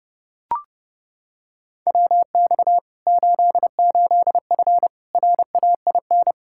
Morse code, heard as a keyed tone, about 32 wpm, 700 Hz. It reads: WX 88F RAIN
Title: Practice QSO - Exchange 2 - Info - 30wpm